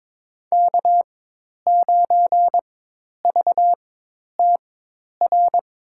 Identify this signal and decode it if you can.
Morse code, heard as a keyed tone, about 22 wpm, spaced as for 10 wpm, 700 Hz. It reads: K9VTR
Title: USA Call Signs - 5 Characters - 4x Character Spacing - 22wpm